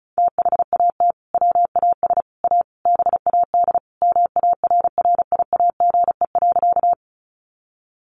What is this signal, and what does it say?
Morse code, heard as a keyed tone, about 34 wpm, 700 Hz. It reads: THAT WAS A BAD MARRIAGE.
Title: Sentences from Top 800 Words 35wpm